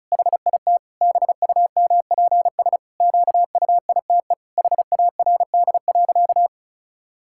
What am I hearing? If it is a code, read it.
Morse code, heard as a keyed tone, about 36 wpm, 700 Hz. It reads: HIT BUMPS QUITE HARD.
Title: Pangram Challenge v2 - 35wpm